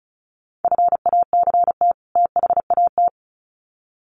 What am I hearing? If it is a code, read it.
Morse code, heard as a keyed tone, about 34 wpm, 700 Hz. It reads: FACT THAT
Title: Sets of 2 Words 35wpm